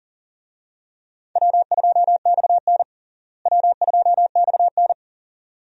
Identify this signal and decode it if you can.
Morse code, heard as a keyed tone, about 37 wpm, 700 Hz. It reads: W2XN W2XN
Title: USA Call Signs - 4 Characters - Repeated 2x - 40wpm